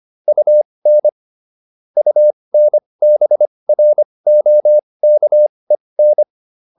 Morse code, heard as a keyed tone, about 25 wpm, spaced as for 20 wpm, 600 Hz. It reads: UN UNBROKEN